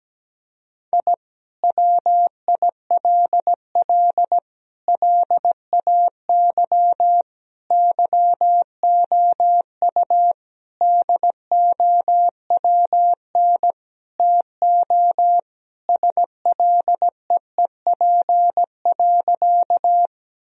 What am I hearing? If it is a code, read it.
Morse code, heard as a keyed tone, about 17 wpm, 700 Hz. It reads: I WILL LAY YOU DOWN TO SLEEP.
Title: Sentences from Top 1000 Words - Review Part 2 - 17wpm